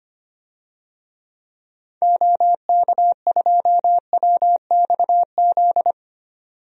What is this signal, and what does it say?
Morse code, heard as a keyed tone, about 25 wpm, 700 Hz. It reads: OK2WXZ
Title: DXCC-100 Call Signs - 6 Characters - 25wpm